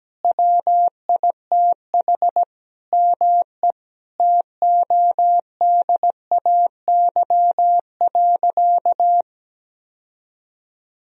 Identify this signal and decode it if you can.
Morse code, heard as a keyed tone, about 17 wpm, 700 Hz. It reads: WITH ME TODAY.
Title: Sentences from Top 1000 Words - Review Part 4 - 17wpm